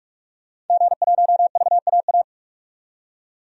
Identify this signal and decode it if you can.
Morse code, heard as a keyed tone, about 47 wpm, 700 Hz. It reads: G1VAA
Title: DXCC-100 Call Signs - 5 Characters - 45wpm